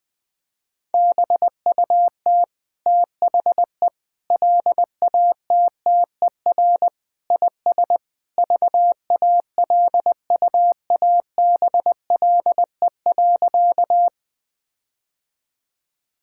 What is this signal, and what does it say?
Morse code, heard as a keyed tone, about 20 wpm, 700 Hz. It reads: BUT THE LATTER IS VALUABLE.
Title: Sentences from Top 1600 Words 20wpm